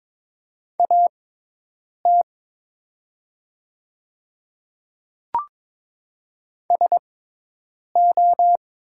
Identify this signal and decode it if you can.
Morse code, heard as a keyed tone, about 22 wpm, spaced as for 8 wpm, 700 Hz. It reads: AT SO
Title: 6x Character Spacing - Top 100 Words - 22wpm